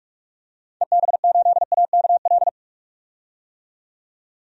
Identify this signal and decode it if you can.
Morse code, heard as a keyed tone, about 45 wpm, 700 Hz. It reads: EB8AKL
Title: DXCC-100 Call Signs - 6 Characters - 45wpm